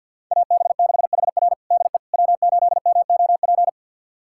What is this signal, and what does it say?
Morse code, heard as a keyed tone, about 51 wpm, 700 Hz. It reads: AB6HR DE W8MOP
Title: USA Callsign DE Callsign - 5 Characters - 50wpm